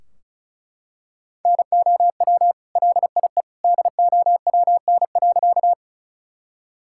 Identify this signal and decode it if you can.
Morse code, heard as a keyed tone, about 34 wpm, 700 Hz. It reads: NOW LIE DOWN.
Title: Sentences from Top 1600 Words 35wpm